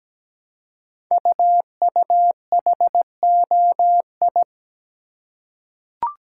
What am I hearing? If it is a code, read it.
Morse code, heard as a keyed tone, about 17 wpm, 700 Hz. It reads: UUHOI
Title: Code Groups - 5 Letters - 17wpm